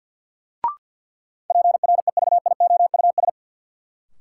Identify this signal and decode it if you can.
Morse code, heard as a keyed tone, about 53 wpm, 700 Hz. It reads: PREVIOUS
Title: Sentences from Top 1500 Words - New Word Drill Alt - 50wpm